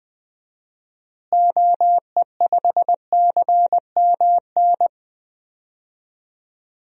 Morse code, heard as a keyed tone, about 20 wpm, 700 Hz. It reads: OE5CMN
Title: DXCC-100 Call Signs - 6 Characters - Repeated 2x - 20wpm